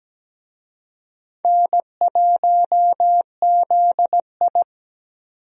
Morse code, heard as a keyed tone, about 17 wpm, 700 Hz. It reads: N1ZI
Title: USA Call Signs - 4 Characters - 17wpm